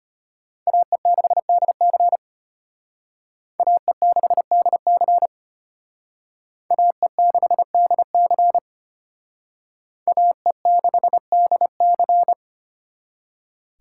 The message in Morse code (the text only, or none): AE6DC AE6DC AE6DC AE6DC